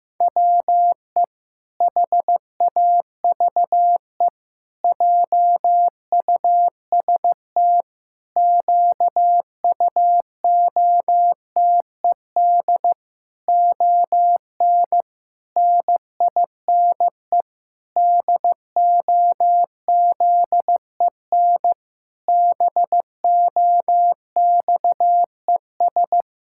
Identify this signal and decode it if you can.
Morse code, heard as a keyed tone, about 14 wpm, 700 Hz. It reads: WE HAVE JUST QUOTED ON NINE DOZEN BOXES
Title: Pangram Challenge v2 - 15wpm